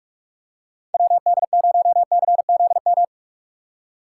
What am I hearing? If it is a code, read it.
Morse code, heard as a keyed tone, about 49 wpm, 700 Hz. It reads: WD0CZM